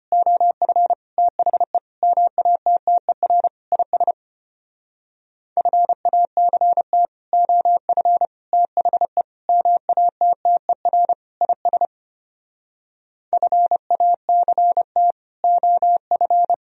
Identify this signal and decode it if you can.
Morse code, heard as a keyed tone, about 33 wpm, 700 Hz. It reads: OF THE MATTER IS FACT OF THE MATTER IS FACT OF